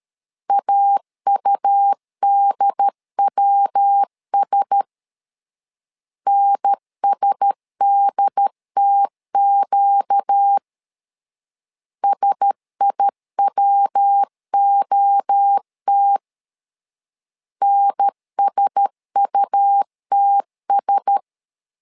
Morse code, heard as a keyed tone, about 13 wpm, 800 Hz. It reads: AUDWS NSDTQ SIWOT NSUTS
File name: LEZIONI / 4 Q1S 12wpm 1802